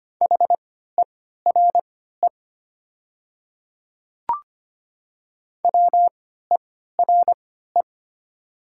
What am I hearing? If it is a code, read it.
Morse code, heard as a keyed tone, about 25 wpm, spaced as for 14 wpm, 690 Hz. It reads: HERE WERE
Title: 3x Character Spacing - Top 300 Words - 25wpm